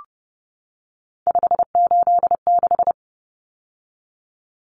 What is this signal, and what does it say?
Morse code, heard as a keyed tone, about 28 wpm, 700 Hz. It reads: 586
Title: Common Cut Numbers - 3-Digit 30wpm